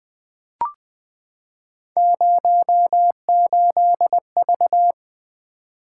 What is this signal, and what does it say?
Morse code, heard as a keyed tone, about 20 wpm, 700 Hz. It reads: 08V
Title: Three Letter-Number Combinations - 20wpm